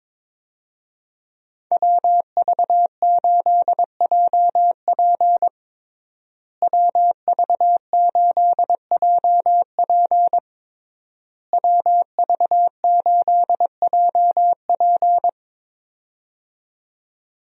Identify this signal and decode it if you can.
Morse code, heard as a keyed tone, about 21 wpm, 700 Hz. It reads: WV8JP WV8JP WV8JP
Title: USA Call Signs - 5 Characters - Repeated 3x - 22wpm